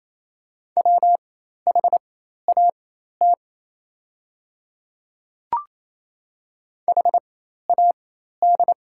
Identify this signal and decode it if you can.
Morse code, heard as a keyed tone, about 28 wpm, spaced as for 13 wpm, 700 Hz. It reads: WHAT HAD
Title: 4x Character Spacing - Top 100 Words - 28wpm